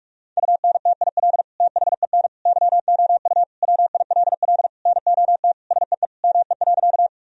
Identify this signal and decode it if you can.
Morse code, heard as a keyed tone, about 42 wpm, 700 Hz. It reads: UNTIL THEN YOU WILL NOT SEE ME.
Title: Sentences from Top 1000 Words - Review Part 1 - 45wpm